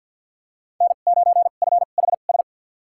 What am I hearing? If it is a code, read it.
Morse code, heard as a keyed tone, about 53 wpm, spaced as for 37 wpm, 700 Hz. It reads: N9FHS
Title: USA Call Signs - 2x Character Spacing - 50wpm